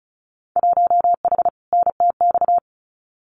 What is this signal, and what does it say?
Morse code, heard as a keyed tone, about 34 wpm, 700 Hz. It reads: 1H NTX
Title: Winter Field Day Exchanges Encore 35wpm